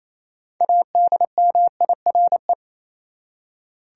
Morse code, heard as a keyed tone, about 28 wpm, 700 Hz. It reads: ADMIRE
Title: Six Letter Words - 28wpm